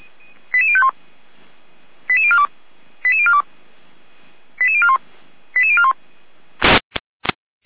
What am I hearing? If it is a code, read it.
Five-tone selective call (ZVEI), heard as quick runs of five tones, tones of about 70 ms. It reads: 88051 88042 88042 88041 88041
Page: audio.f85360.feuerwehr-orlamuende.org